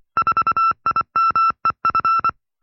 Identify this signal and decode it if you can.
Morse code, heard as a keyed tone, about 24 wpm, 1340 Hz. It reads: 4IMEF
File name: FreeFiles / CW-practice / 20WPM-side2